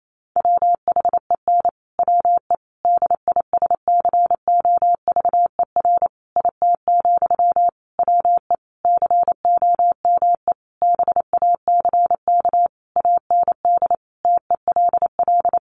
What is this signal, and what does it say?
Morse code, heard as a keyed tone, about 28 wpm, 700 Hz. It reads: WHEN WE DISCOVER IT, WE COME BACK AND TELL
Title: Sentences from Top 1100 Words 28wpm